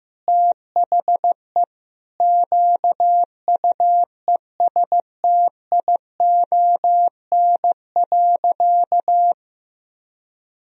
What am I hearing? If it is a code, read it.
Morse code, heard as a keyed tone, about 15 wpm, 700 Hz. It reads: THE QUESTION.
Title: Sentences from Top 1000 Words - Review Part 1 - 15wpm